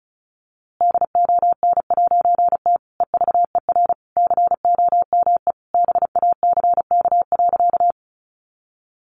Morse code, heard as a keyed tone, about 34 wpm, 700 Hz. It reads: DON'T EVER COME BACK.
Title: A Sound Of Thunder 35wpm